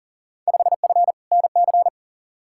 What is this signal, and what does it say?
Morse code, heard as a keyed tone, about 40 wpm, 700 Hz. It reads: 5F NC
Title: Field Day Exchanges Encore - 40wpm